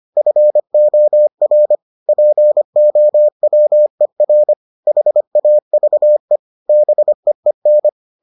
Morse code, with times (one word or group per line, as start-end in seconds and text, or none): 0.2-1.8 FOR
2.1-4.5 POWER
4.9-6.4 HAVE
6.7-7.9 BEEN